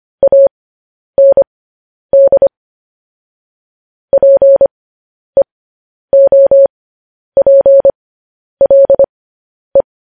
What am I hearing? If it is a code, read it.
Morse code, heard as a keyed tone, about 25 wpm, spaced as for 10 wpm, 550 Hz. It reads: AND PEOPLE
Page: news headlines in morse code at 10 wpm